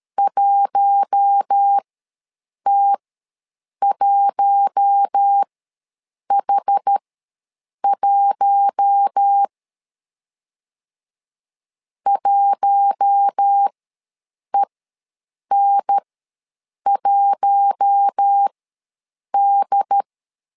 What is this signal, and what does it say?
Morse code, heard as a keyed tone, about 13 wpm, spaced as for 7 wpm, 800 Hz. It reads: 1T1H1 1EN1D
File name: LEZIONI / 4 Q1 ws 12wpm 2129